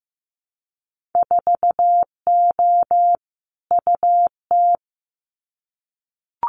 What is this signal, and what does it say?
Morse code, heard as a keyed tone, about 15 wpm, 700 Hz. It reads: 4O UT
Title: Winter Field Day Exchanges Encore 15wpm